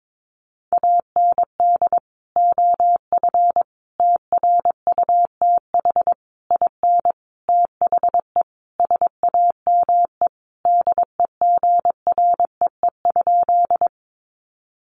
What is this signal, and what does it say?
Morse code, heard as a keyed tone, about 22 wpm, 700 Hz. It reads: AND OF TRUTH IN THE SAME DEGREE?
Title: Sentences from Top 1500 Words 22wpm